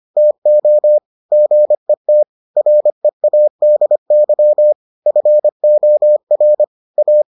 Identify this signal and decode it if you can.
Morse code, heard as a keyed tone, about 25 wpm, 600 Hz. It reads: TO GET READY FOR A